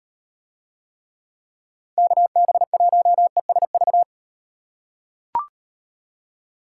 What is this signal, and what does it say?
Morse code, heard as a keyed tone, about 38 wpm, 700 Hz. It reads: KB1ESV